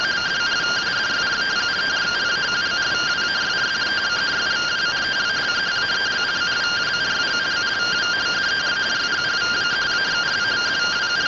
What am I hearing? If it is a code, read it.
Two-tone teleprinter signal, about 170 Hz shift.